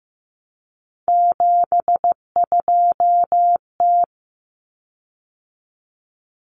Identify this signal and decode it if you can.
Morse code, heard as a keyed tone, about 15 wpm, 700 Hz. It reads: 72T
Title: Common Cut Numbers - 3-Digit 15wpm